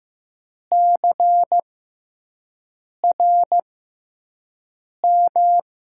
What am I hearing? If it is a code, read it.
Morse code, heard as a keyed tone, about 15 wpm, spaced as for 5 wpm, 700 Hz. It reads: CRM
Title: USA Call Signs - 5 Characters - 6x Character Spacing - 15wpm